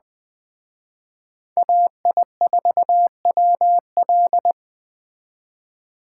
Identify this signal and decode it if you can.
Morse code, heard as a keyed tone, about 20 wpm, 700 Hz. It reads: AI4WL
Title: Speed Racing - USA 2x2 Call Signs 17wpm